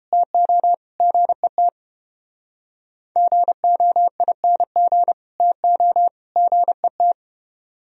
Morse code, heard as a keyed tone, about 29 wpm, 700 Hz. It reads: TO GET GOING TO GET